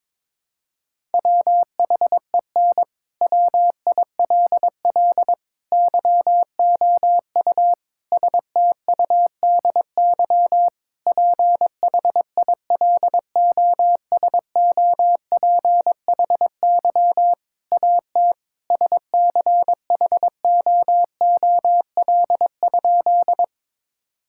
Morse code, heard as a keyed tone, about 22 wpm, 700 Hz. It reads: WHEN WILL YOU STUDY PHILOSOPHY AT SCHOOL?
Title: Sentences from Top 1000 Words - Review Part 6 - 22wpm